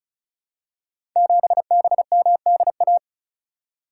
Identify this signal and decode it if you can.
Morse code, heard as a keyed tone, about 36 wpm, 690 Hz. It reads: 7BMDA
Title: Code Groups - 5 Letter:Numbers - 35wpm